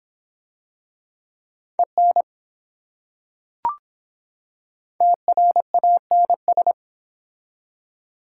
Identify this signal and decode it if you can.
Morse code, heard as a keyed tone, about 25 wpm, 700 Hz. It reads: EN TRANS